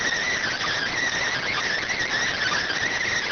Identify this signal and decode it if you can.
Emergency Alert System (SAME) tones, heard as harsh buzzing bursts. It is heard from 0.2 to 2.8 s.